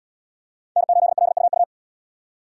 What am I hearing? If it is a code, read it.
Morse code, heard as a keyed tone, about 70 wpm, 700 Hz. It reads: N8LUU